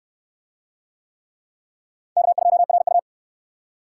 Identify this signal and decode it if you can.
Morse code, heard as a keyed tone, about 68 wpm, 700 Hz. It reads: K2RU